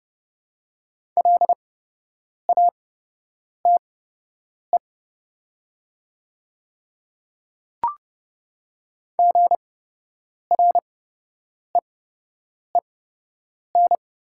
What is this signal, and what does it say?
Morse code, heard as a keyed tone, about 30 wpm, spaced as for 8 wpm, 700 Hz. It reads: LATE GREEN
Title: 8x Character Spacing - Top 500 Words - 30wpm